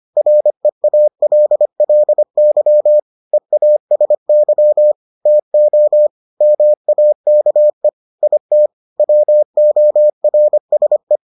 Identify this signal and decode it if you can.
Morse code, heard as a keyed tone, about 25 wpm, 600 Hz. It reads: REALLY EASY TO MAKE IT WORSE